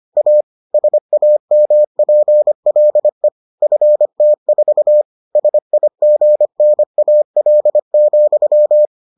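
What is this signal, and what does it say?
Morse code, heard as a keyed tone, about 25 wpm, 600 Hz. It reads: A SAMPLE FT4 SIGNAL,
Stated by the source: Morse edition of the Foundations of Amateur Radio episodes.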